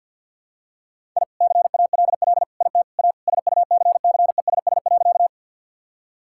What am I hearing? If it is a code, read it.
Morse code, heard as a keyed tone, about 47 wpm, 700 Hz. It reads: I CALL IT A SUCCESS.
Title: Sentences from Top 1000 Words - Review Part 6 - 50wpm